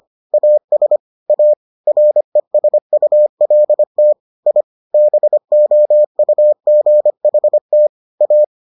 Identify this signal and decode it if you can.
Morse code, heard as a keyed tone, about 25 wpm, 600 Hz. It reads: AS A RESULT I BOUGHT A